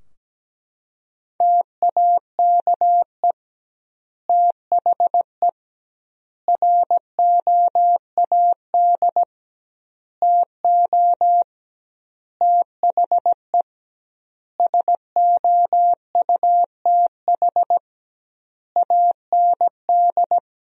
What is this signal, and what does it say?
Morse code, heard as a keyed tone, about 17 wpm, 700 Hz. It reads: TAKE THE ROAD TO THE SOUTH AND
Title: 2x Word-Spacing - Sentences from Top 400 Words - 17wpm